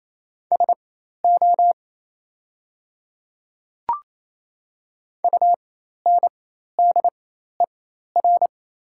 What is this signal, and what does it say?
Morse code, heard as a keyed tone, about 27 wpm, spaced as for 13 wpm, 700 Hz. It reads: SO UNDER
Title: 4x Character Spacing - Top 300 Words - 28wpm